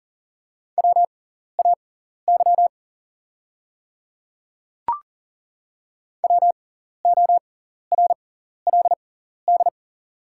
Morse code, heard as a keyed tone, about 38 wpm, spaced as for 14 wpm, 700 Hz. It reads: WAY WORLD
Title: 6x Character Spacing - Top 200 Words - 40wpm